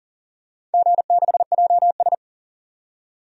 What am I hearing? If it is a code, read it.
Morse code, heard as a keyed tone, about 40 wpm, 700 Hz. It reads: G6JS
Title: DXCC-100 Call Signs - 4 Characters - 40wpm